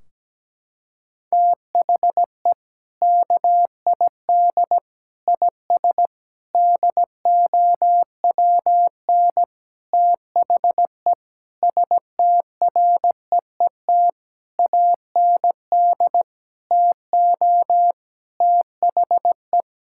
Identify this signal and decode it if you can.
Morse code, heard as a keyed tone, about 17 wpm, 700 Hz. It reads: THE KID IS DOWN THE STREET AND TO THE
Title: Sentences from Top 1600 Words 17wpm